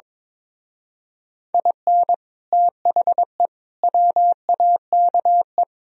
Morse code, heard as a keyed tone, about 22 wpm, 700 Hz. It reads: IN THE WAKE
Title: Speed-Racing - Sets of 5 Words 22wpm